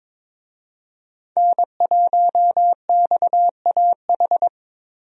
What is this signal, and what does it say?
Morse code, heard as a keyed tone, about 22 wpm, 700 Hz. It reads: N1XAH
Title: USA Call Signs - 5 Characters - Repeated 2x - 22wpm